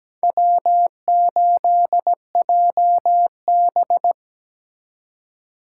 Morse code, heard as a keyed tone, about 17 wpm, 700 Hz. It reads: W8JB
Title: USA Call Signs - 4 Characters - 8x Character Spacing - 17wpm